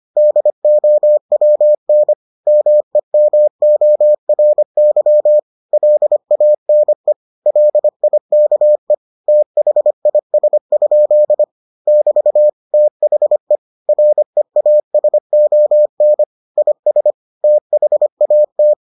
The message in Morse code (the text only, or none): DOWN MEMORY LANE LIKE THIS? <BT> THE REASON IS THAT